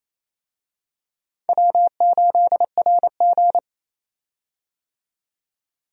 Morse code, heard as a keyed tone, about 28 wpm, 700 Hz. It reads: W8RG
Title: USA Call Signs - 4 Characters - 28wpm